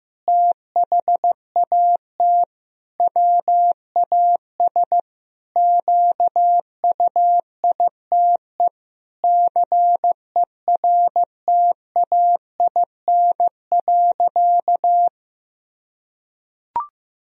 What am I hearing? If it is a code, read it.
Morse code, heard as a keyed tone, about 15 wpm, 700 Hz. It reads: THAT WAS QUITE CERTAIN.
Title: Sentences from Top 1000 Words - Review Part 2 - 15wpm